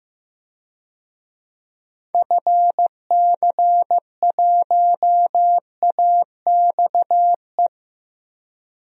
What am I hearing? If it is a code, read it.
Morse code, heard as a keyed tone, about 15 wpm, 700 Hz. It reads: FC1AXE
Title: DXCC-100 Call Signs - 6 Characters - Repeated 3x - 15wpm